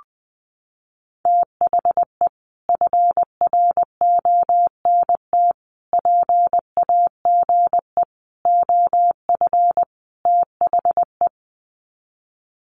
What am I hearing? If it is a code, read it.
Morse code, heard as a keyed tone, about 20 wpm, 700 Hz. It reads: THE FRONT PAGE OF THE
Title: Sets of 5 Words Encore 20wpm